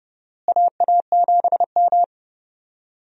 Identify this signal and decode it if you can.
Morse code, heard as a keyed tone, about 30 wpm, 700 Hz. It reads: AA7M
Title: USA Call Signs - 4 Characters - Repeated 3x - 30wpm